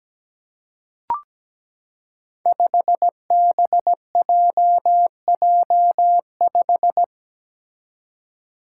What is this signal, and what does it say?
Morse code, heard as a keyed tone, about 17 wpm, 700 Hz. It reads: 5BJJ5
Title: Five Challenging Letter-Number Combinations 17wpm